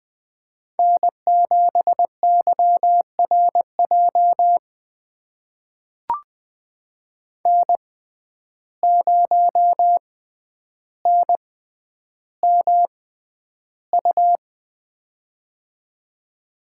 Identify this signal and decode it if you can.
Morse code, heard as a keyed tone, about 21 wpm, spaced as for 10 wpm, 700 Hz. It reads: N7YRJ N0NMU
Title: USA Call Signs - 5 Characters - 6x Character Spacing - 20wpm